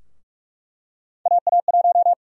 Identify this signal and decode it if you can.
Morse code, heard as a keyed tone, about 44 wpm, 700 Hz. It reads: AA1
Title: USA Call Sign Prefixes - Repeated 2x - 45wpm